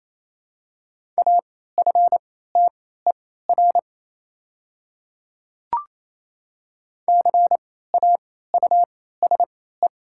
Morse code, heard as a keyed tone, about 27 wpm, spaced as for 16 wpm, 700 Hz. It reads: AFTER CAUSE
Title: 3x Character Spacing - Top 200 Words - 28wpm